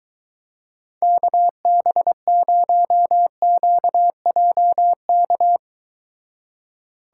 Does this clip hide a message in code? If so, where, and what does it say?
Yes: Morse code from 1.0 to 5.6 s, heard as a keyed tone, about 23 wpm, 700 Hz. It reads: KB0QJK